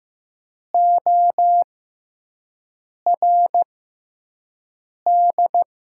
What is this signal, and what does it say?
Morse code, heard as a keyed tone, about 15 wpm, spaced as for 5 wpm, 700 Hz. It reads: ORD
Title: 6x Character Spacing - Top 100 Words - 15wpm